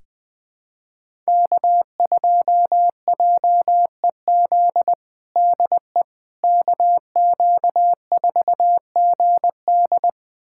Morse code, heard as a keyed tone, about 21 wpm, 700 Hz. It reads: K2JEZ DE KQ4GD
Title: USA Callsign DE Callsign - 5 Characters - 20wpm